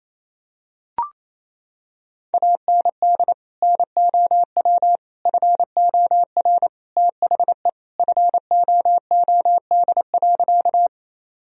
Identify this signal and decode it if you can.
Morse code, heard as a keyed tone, about 28 wpm, 700 Hz. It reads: AND NOW FOR THE FOOD.
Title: Sentences from Top 1000 Words - Review Part 1 - 28wpm